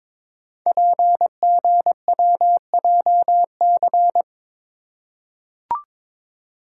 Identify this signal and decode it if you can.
Morse code, heard as a keyed tone, about 22 wpm, 700 Hz. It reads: PGWJC
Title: Code Groups - 5 Letters - 22wpm